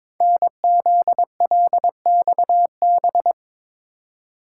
Morse code, heard as a keyed tone, about 23 wpm, 700 Hz. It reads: NZLXB
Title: Code Groups - 5 Letters - 22wpm